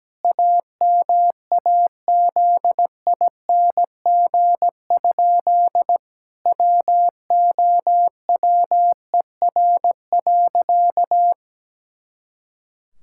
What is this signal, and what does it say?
Morse code, heard as a keyed tone, about 17 wpm, 700 Hz. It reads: AMAZING? WOWER.
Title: Pangram Challenge v2 - 17wpm